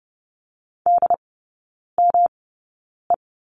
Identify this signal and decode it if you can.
Morse code, heard as a keyed tone, about 30 wpm, spaced as for 9 wpm, 700 Hz. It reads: DME
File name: Lesson 18e - Letter m - Three Characters 30wpm